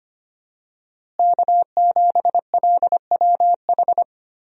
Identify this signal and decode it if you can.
Morse code, heard as a keyed tone, about 25 wpm, 700 Hz. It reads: K7LWH
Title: USA Call Signs - 5 Characters - Repeated 3x - 25wpm